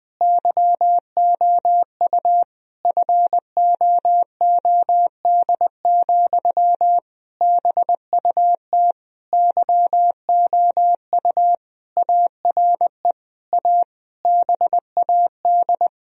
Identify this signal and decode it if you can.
Morse code, heard as a keyed tone, about 21 wpm, 700 Hz. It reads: YOU FOOD, BUT YOU ARE A BAD
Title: Sentences from Top 1000 Words - Review Part 1 - 20wpm